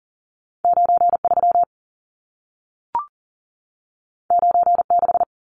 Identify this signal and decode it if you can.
Morse code, heard as a keyed tone, about 40 wpm, 700 Hz. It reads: 93 96
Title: Numbers - 2-Digit 40wpm